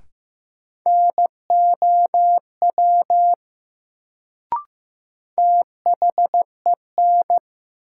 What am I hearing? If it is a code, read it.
Morse code, heard as a keyed tone, about 15 wpm, 700 Hz. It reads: NOW THEN
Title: Sets of 2 Words - New Word Drill - 15wpm